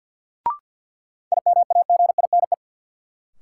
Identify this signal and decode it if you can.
Morse code, heard as a keyed tone, about 46 wpm, 700 Hz. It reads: IMAGINE
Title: Sentences from Top 1400 Words - New Word Drill Alt - 50wpm